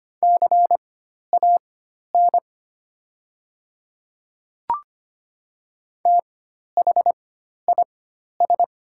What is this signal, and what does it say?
Morse code, heard as a keyed tone, about 25 wpm, spaced as for 12 wpm, 700 Hz. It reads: CAN THIS
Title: 4x Character Spacing - Top 100 Words - 25wpm